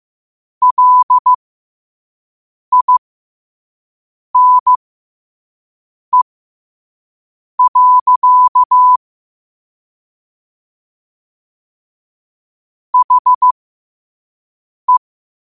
Morse code, heard as a keyed tone, about 15 wpm, spaced as for 5 wpm, 1000 Hz. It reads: LINE. HE
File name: 5wpm